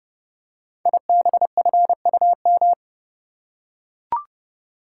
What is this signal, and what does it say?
Morse code, heard as a keyed tone, about 28 wpm, 700 Hz. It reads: IBFUM
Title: Code Groups - 5 Letter:Numbers - 30wpm